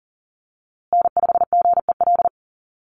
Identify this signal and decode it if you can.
Morse code, heard as a keyed tone, about 40 wpm, 700 Hz. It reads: N5GEL
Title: USA Call Signs Repeated 2x 40wpm